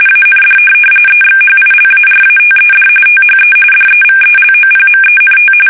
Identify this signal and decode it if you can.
RTTY, about 45.45 baud. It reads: AND EXHILARATED THE DEVELOPMENT
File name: fsk850 / benjamin-g-lamme